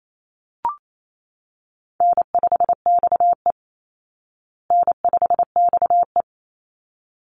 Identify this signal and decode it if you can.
Morse code, heard as a keyed tone, about 28 wpm, 700 Hz. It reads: N5XE N5XE
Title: Easy Call Signs Repeated 2x - 28wpm